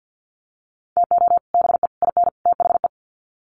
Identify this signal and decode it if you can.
Morse code, heard as a keyed tone, about 50 wpm, 700 Hz. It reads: TO BE IN THE
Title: Sets of 4 Words 50wpm